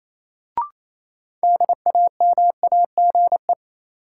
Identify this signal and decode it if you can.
Morse code, heard as a keyed tone, about 28 wpm, 700 Hz. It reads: DAMAGE
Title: Sentences from Top 1100 Words - New Word Drill Alt - 28wpm